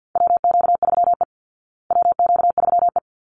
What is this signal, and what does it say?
Morse code, heard as a keyed tone, about 50 wpm, 700 Hz. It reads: WQ3E WQ3E
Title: Call Signs Repeated 3x - 50wpm